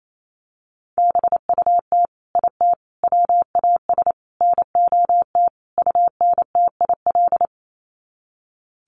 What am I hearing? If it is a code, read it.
Morse code, heard as a keyed tone, about 28 wpm, 700 Hz. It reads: BUT IT WAS NOT UNTIL
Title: Sets of 5 Words Encore2 28wpm